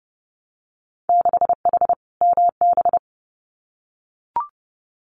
Morse code, heard as a keyed tone, about 32 wpm, 700 Hz. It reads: 6H MB
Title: Winter Field Day Exchanges Encore 30wpm